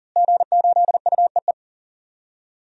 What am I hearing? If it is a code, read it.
Morse code, heard as a keyed tone, about 40 wpm, 700 Hz. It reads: G8UEE
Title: DXCC-100 Call Signs - 5 Characters - Repeated 3x - 40wpm